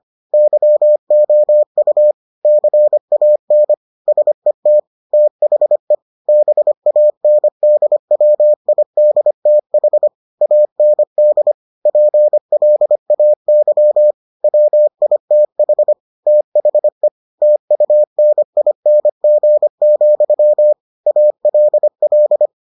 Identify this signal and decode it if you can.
Morse code, heard as a keyed tone, about 25 wpm, 600 Hz. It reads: YOU CAN SET THE BANDWIDTH AND PLAY WITH THE TUNING, ALL